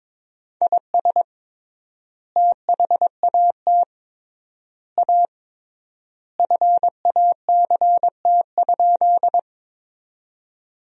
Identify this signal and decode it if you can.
Morse code, heard as a keyed tone, about 22 wpm, 700 Hz. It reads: IS THAT A FACT?
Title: 3x Word-Spacing - Sentences from Top 500 Words - 22wpm